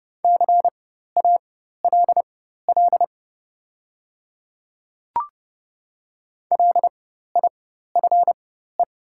Morse code, heard as a keyed tone, about 30 wpm, spaced as for 14 wpm, 700 Hz. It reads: CALL LIFE
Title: 4x Character Spacing - Top 300 Words - 30wpm